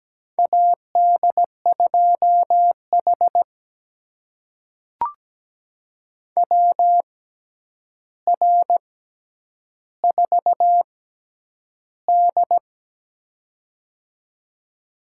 Morse code, heard as a keyed tone, about 17 wpm, spaced as for 9 wpm, 700 Hz. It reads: AD2H WR4D
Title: USA Call Signs - 4 Characters - 6x Character Spacing - 17wpm